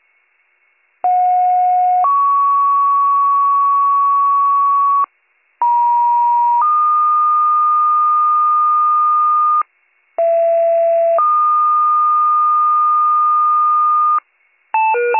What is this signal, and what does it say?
Two-tone page, heard as a one-second tone, then a three-second tone, about 730 then 1120 Hz, about 950 then 1220 Hz, about 670 then 1220 Hz. It is heard from 1.0 to 14.2 s.